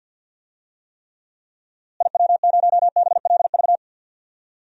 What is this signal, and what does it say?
Morse code, heard as a keyed tone, about 52 wpm, 700 Hz. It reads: IW0BLV
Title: DXCC-100 Call Signs - 6 Characters - Repeated 2x - 50wpm